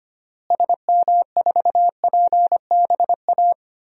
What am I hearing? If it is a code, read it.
Morse code, heard as a keyed tone, about 25 wpm, 700 Hz. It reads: SM4PBA
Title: DXCC-100 Call Signs - 6 Characters - Repeated 3x - 25wpm